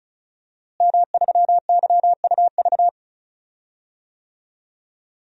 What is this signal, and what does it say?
Morse code, heard as a keyed tone, about 36 wpm, 700 Hz. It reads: M3YUV
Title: DXCC-100 Call Signs - 5 Characters - Repeated 2x - 35wpm